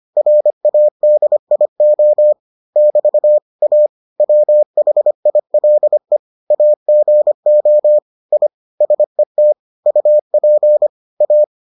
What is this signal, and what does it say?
Morse code, heard as a keyed tone, about 25 wpm, 600 Hz. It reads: RADIO <BT> A WHILE AGO I SET UP A